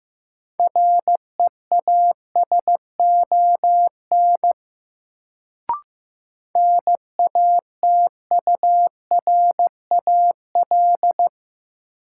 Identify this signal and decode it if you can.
Morse code, heard as a keyed tone, about 15 wpm, 700 Hz. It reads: REASON NATURAL
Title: Sentences from Top 700 Words - New Word Drill - 15wpm